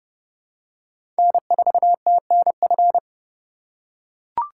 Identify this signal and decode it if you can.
Morse code, heard as a keyed tone, about 31 wpm, 700 Hz. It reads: N4TNF
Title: USA Call Signs - 4x Character Spacing - 30wpm